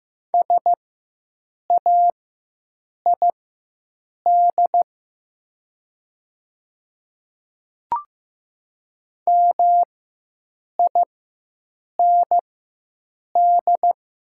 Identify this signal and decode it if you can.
Morse code, heard as a keyed tone, about 15 wpm, spaced as for 7 wpm, 700 Hz. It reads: SAID MIND
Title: 4x Character Spacing - Top 500 Words - 15wpm